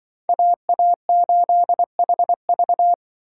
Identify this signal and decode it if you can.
Morse code, heard as a keyed tone, about 24 wpm, 700 Hz. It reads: AA8HV
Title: Speed Racing - USA 2x2 Call Signs 20wpm